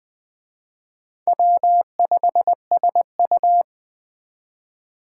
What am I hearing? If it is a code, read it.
Morse code, heard as a keyed tone, about 20 wpm, 700 Hz. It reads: W5SU
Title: Speed Racing - USA 1x2 Call Signs 17wpm